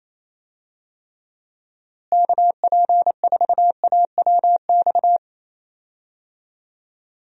Morse code, heard as a keyed tone, about 28 wpm, 700 Hz. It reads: KP4AWX
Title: DXCC-100 Call Signs - 6 Characters - Repeated 3x - 28wpm